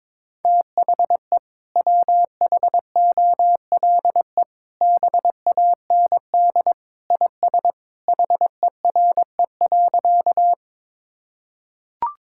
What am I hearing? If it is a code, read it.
Morse code, heard as a keyed tone, about 22 wpm, 700 Hz. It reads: THE WHOLE BAND IS HERE.
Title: Sentences from Top 1600 Words 22wpm